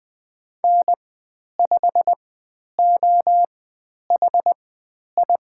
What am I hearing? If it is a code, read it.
Morse code, heard as a keyed tone, about 20 wpm, spaced as for 10 wpm, 700 Hz. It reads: N5OHI
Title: 095 Lesson 16g - Letter c - Call Signs 20wpm